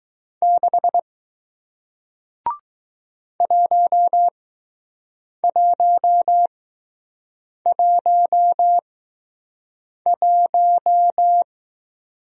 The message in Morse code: 6 1111

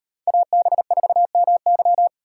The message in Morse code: AB4MY

